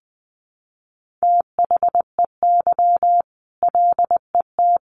Morse code, heard as a keyed tone, about 20 wpm, 700 Hz. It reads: THEY LET
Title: A Sound Of Thunder 20wpm